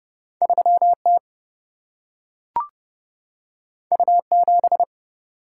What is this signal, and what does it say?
Morse code, heard as a keyed tone, about 30 wpm, 700 Hz. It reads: 3 T U 7
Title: Single-Letter Single-Number 30wpm